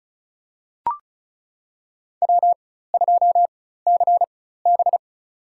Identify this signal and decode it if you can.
Morse code, heard as a keyed tone, about 36 wpm, spaced as for 16 wpm, 700 Hz. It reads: W2CB
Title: USA Call Signs - 4 Characters - 4x Character Spacing - 35wpm